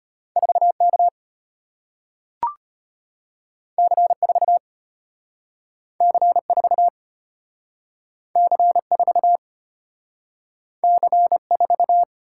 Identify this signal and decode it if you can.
Morse code, heard as a keyed tone, about 34 wpm, 700 Hz. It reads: 4K C4 C4 C4 C4